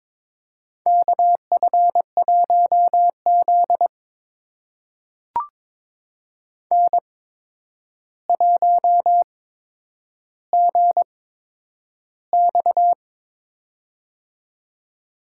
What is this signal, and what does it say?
Morse code, heard as a keyed tone, about 22 wpm, spaced as for 9 wpm, 700 Hz. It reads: KF1Z N1GX
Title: USA Call Signs - 4 Characters - 8x Character Spacing - 22wpm